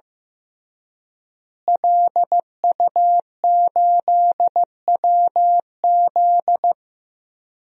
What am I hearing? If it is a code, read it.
Morse code, heard as a keyed tone, about 15 wpm, 700 Hz. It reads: LU8WZ
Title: DXCC-100 Call Signs - 5 Characters - Repeated 3x - 15wpm